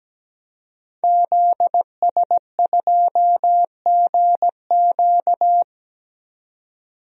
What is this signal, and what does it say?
Morse code, heard as a keyed tone, about 17 wpm, 700 Hz. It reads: ZS2GQ
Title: DXCC-100 Call Signs - 5 Characters - 17wpm